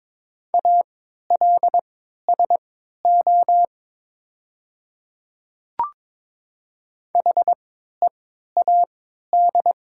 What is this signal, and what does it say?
Morse code, heard as a keyed tone, about 22 wpm, spaced as for 13 wpm, 690 Hz. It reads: ALSO HEAD
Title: 3x Character Spacing - Top 300 Words - 22wpm